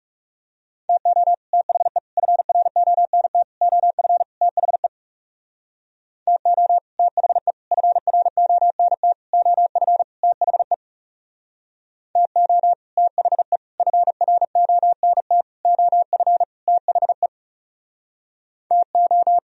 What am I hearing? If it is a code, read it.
Morse code, heard as a keyed tone, about 38 wpm, 700 Hz. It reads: TO THE FRONT OF THE TO THE FRONT OF THE TO THE FRONT OF THE TO